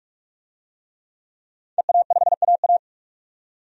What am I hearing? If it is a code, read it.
Morse code, heard as a keyed tone, about 41 wpm, 700 Hz. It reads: EA5AA
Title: DXCC-100 Call Signs - 5 Characters - 45wpm